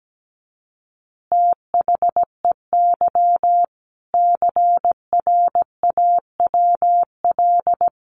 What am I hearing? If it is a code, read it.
Morse code, heard as a keyed tone, about 17 wpm, 700 Hz. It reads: THEY CRAWL
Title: Easy Phrases 17wpm